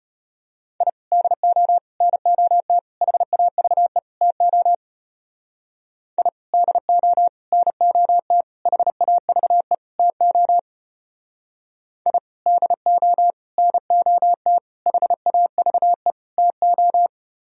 Morse code, heard as a keyed tone, about 32 wpm, 700 Hz. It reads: I DO NOT HAVE TO I DO NOT HAVE TO I DO NOT HAVE TO